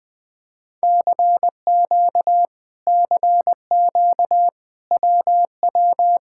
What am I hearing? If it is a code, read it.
Morse code, heard as a keyed tone, about 20 wpm, 700 Hz. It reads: CQ CQ WW
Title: Speed-Racing Common QSO Phrases - 17wpm